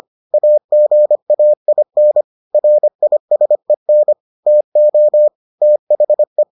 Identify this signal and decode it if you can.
Morse code, heard as a keyed tone, about 25 wpm, 600 Hz. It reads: AGAIN RISEN TO THE